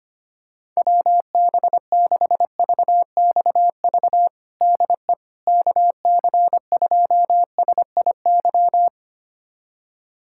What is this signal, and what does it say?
Morse code, heard as a keyed tone, about 25 wpm, 700 Hz. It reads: WB6VXV DE KC2SIY
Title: USA Callsign DE Callsign 25wpm